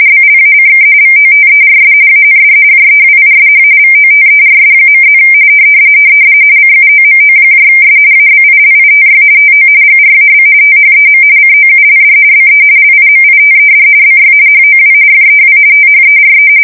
RTTY, about 45.45 baud. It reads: BLACK VOTERS AND JEWISH VOTERS.+ CUOMOS ABILITY TO GRAB THEM COULD BE LIMITED BY HOW FEW THERE
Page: 170 Shift 2125/2295